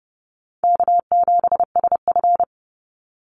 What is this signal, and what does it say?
Morse code, heard as a keyed tone, about 30 wpm, 700 Hz. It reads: K7SF
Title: Easy Call Signs Repeated 3x - 30wpm